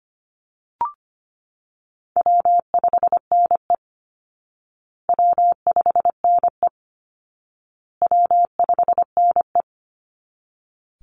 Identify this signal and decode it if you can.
Morse code, heard as a keyed tone, about 25 wpm, 700 Hz. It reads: W5NE W5NE W5NE
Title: Easy Call Signs Repeated 3x - 25wpm